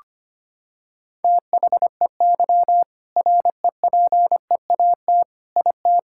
Morse code, heard as a keyed tone, about 25 wpm, 700 Hz. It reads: THEY REPEAT IT
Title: Sentences from Top 1600 Words 25wpm